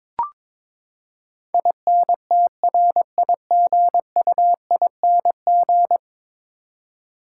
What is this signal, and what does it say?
Morse code, heard as a keyed tone, about 21 wpm, 700 Hz. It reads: INTRIGUING
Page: morse-code-ninja-practice-22wpm